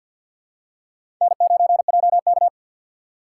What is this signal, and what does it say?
Morse code, heard as a keyed tone, about 52 wpm, 700 Hz. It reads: N9JK
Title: USA Call Signs - 4 Characters - 6x Character Spacing - 50wpm